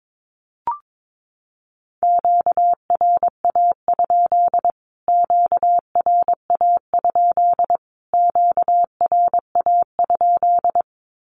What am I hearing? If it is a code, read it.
Morse code, heard as a keyed tone, about 22 wpm, 700 Hz. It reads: QRA? QRA? QRA?
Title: Q-Codes Repeated 3x 22wpm